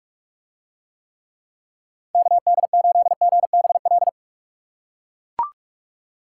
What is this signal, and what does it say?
Morse code, heard as a keyed tone, about 42 wpm, 700 Hz. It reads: KD8GBL